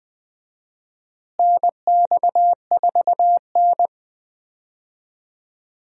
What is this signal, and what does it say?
Morse code, heard as a keyed tone, about 20 wpm, 700 Hz. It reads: NX4N
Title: USA Call Signs - 4 Characters - Repeated 3x - 20wpm